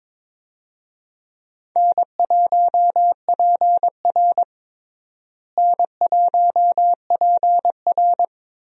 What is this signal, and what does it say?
Morse code, heard as a keyed tone, about 22 wpm, 700 Hz. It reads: N1PR N1PR
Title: USA Call Signs - 4 Characters - Repeated 2x - 22wpm